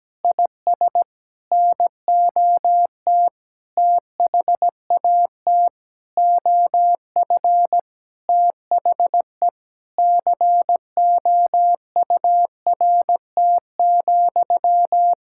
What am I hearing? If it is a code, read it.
Morse code, heard as a keyed tone, about 17 wpm, 700 Hz. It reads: IS NOT THAT OF THE COURT,
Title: Sentences from Top 1600 Words 17wpm